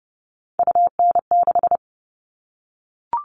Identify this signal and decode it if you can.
Morse code, heard as a keyed tone, about 30 wpm, 700 Hz. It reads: UN6
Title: All Cut Numbers - 3-Digit 30wpm